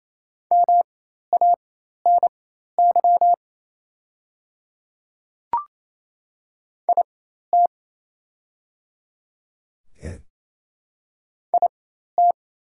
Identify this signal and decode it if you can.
Morse code, heard as a keyed tone, about 28 wpm, spaced as for 13 wpm, 700 Hz. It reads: MANY IT IT
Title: 4x Character Spacing - Top 200 Words - 28wpm